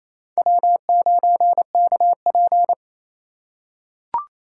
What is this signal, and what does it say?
Morse code, heard as a keyed tone, about 27 wpm, 700 Hz. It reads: W9KP
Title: USA Call Signs - 3x Character Spacing - 28wpm